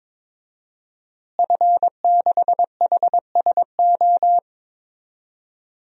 Morse code, heard as a keyed tone, about 22 wpm, 700 Hz. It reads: F6HSO
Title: DXCC-100 Call Signs - 5 Characters - Repeated 3x - 22wpm